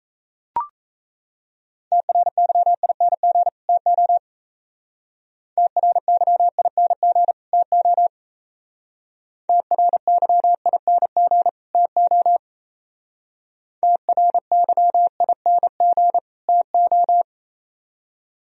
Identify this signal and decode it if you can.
Morse code, heard as a keyed tone, about 37 wpm, 700 Hz. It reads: TRYING TO TRYING TO TRYING TO TRYING TO